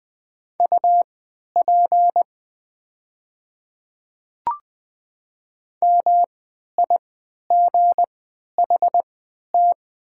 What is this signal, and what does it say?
Morse code, heard as a keyed tone, about 20 wpm, spaced as for 11 wpm, 700 Hz. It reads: UP MIGHT
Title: 3x Character Spacing - Top 300 Words - 20wpm